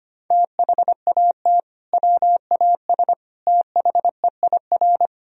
Morse code, heard as a keyed tone, about 25 wpm, 700 Hz. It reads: THAT WAS THEIR
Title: Sentences from Top 1000 Words - Review Part 2 - 25wpm